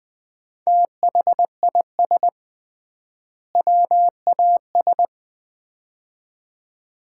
Morse code, heard as a keyed tone, about 21 wpm, 700 Hz. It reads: THIS WAS
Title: 3x Word-Spacing - Sets of 2 Words - 20wpm